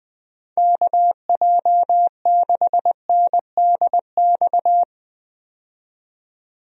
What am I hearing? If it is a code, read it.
Morse code, heard as a keyed tone, about 20 wpm, 700 Hz. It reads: KJ6NDX
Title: USA Call Signs - 6 Characters - 20wpm